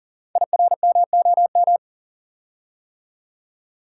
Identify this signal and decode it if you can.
Morse code, heard as a keyed tone, about 40 wpm, 700 Hz. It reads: IRMOM
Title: Code Groups - 5 Letters - 40wpm